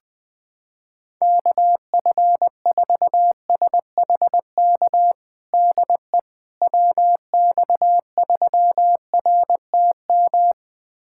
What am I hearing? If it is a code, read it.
Morse code, heard as a keyed tone, about 20 wpm, 700 Hz. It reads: KF4SHK DE WX3RTM
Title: USA Callsign DE Callsign - 6 Characters - 20wpm